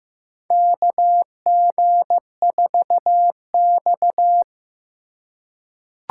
Morse code, heard as a keyed tone, about 15 wpm, 700 Hz. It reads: KG4X
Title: USA Call Signs - 2x Character Spacing - 15wpm